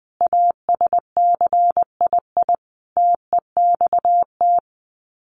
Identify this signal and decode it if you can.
Morse code, heard as a keyed tone, about 20 wpm, 700 Hz. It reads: ASCII TEXT
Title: Sets of 5 Words 20wpm